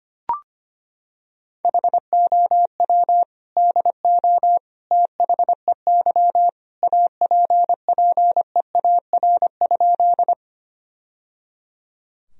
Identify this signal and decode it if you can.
Morse code, heard as a keyed tone, about 25 wpm, 700 Hz. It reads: HOW DO THEY APPEAR?
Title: Sentences from Top 1000 Words - Review Part 2 - 25wpm